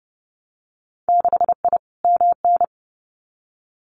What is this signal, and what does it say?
Morse code, heard as a keyed tone, about 30 wpm, 700 Hz. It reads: 6I MN
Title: Winter Field Day Exchanges Encore 30wpm